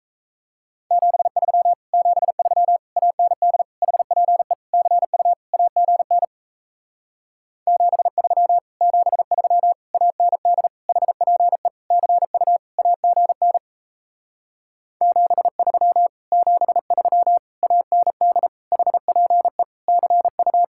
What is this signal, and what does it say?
Morse code, heard as a keyed tone, about 38 wpm, 700 Hz. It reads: 73 73 AND HPE CU AGN 73 73 AND HPE CU AGN 73 73 AND HPE CU